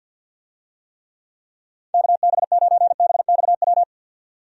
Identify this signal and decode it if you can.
Morse code, heard as a keyed tone, about 47 wpm, 700 Hz. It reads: KB9BXW
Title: USA Call Signs - 6 Characters - 50wpm